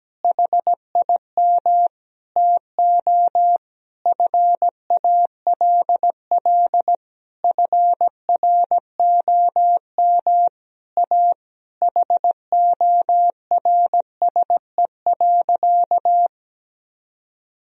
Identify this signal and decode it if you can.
Morse code, heard as a keyed tone, about 17 wpm, 700 Hz. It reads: HIM TO FALL FROM A HORSE.
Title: Sentences from Top 1000 Words - Review Part 2 - 17wpm